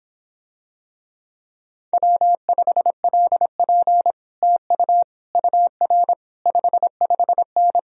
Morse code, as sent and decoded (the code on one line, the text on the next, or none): .-- ..... .-.. .--. / - ..- / ..- .-. / ..... ..... -.
W5LP TU UR 55N